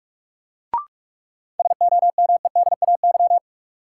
Morse code, heard as a keyed tone, about 42 wpm, 700 Hz. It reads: SOMEDAY
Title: Sentences from Top 1600 Words - New Word Drill - 45wpm